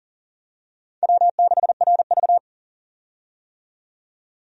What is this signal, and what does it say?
Morse code, heard as a keyed tone, about 40 wpm, 700 Hz. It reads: W6RV
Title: USA Call Signs - 4 Characters - 40wpm